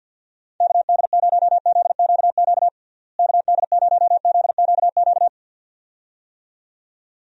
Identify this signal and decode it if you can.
Morse code, heard as a keyed tone, about 48 wpm, 700 Hz. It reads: KD0ZQQ KD0ZQQ
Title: USA Call Signs - 6 Characters - Repeated 2x - 50wpm